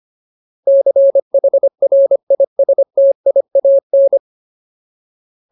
Morse code, heard as a keyed tone, about 25 wpm, 550 Hz. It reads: CHRISTIAN
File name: Ultimate Word List / 25WPM Ultimate